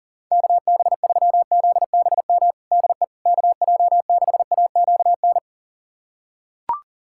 Morse code, heard as a keyed tone, about 40 wpm, 690 Hz. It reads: KB3ZBM DE KJ6AQN
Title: USA Callsign DE Callsign - 6 Characters - 40wpm